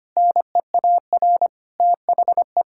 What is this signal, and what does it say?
Morse code, heard as a keyed tone, about 25 wpm, 700 Hz. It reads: NEAR THE
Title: Sentences from Top 1000 Words - Review Part 6 - 25wpm